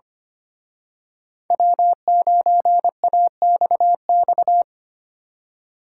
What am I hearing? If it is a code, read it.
Morse code, heard as a keyed tone, about 29 wpm, 700 Hz. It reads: W9AXX
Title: Speed Racing - USA Call Signs 25wpm